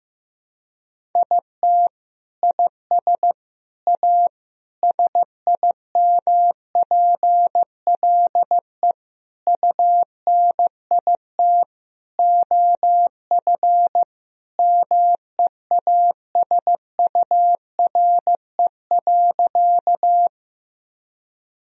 Morse code, heard as a keyed tone, about 15 wpm, 700 Hz. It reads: IT IS A SIMPLE UNIT OF MEASURE.
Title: Sentences from Top 1000 Words - Review Part 2 - 15wpm